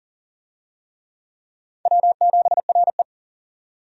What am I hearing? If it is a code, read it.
Morse code, heard as a keyed tone, about 40 wpm, 700 Hz. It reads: W7RE
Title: USA Call Signs - 4 Characters - 40wpm